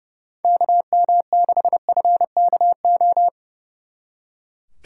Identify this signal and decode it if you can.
Morse code, heard as a keyed tone, about 35 wpm, 700 Hz. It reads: KM6FKO
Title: Speed Racing - USA Call Signs 30wpm